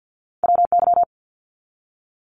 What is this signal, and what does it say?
Morse code, heard as a keyed tone, about 50 wpm, 700 Hz. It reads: WY
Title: ARRL Sections 50wpm